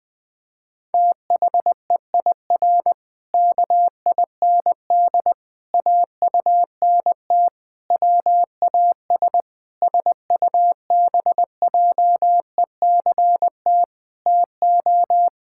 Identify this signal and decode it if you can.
Morse code, heard as a keyed tone, about 20 wpm, 700 Hz. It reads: THEIR KIND AUNT WAS SUBJECT TO
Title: Pangram Challenge v2 - 20wpm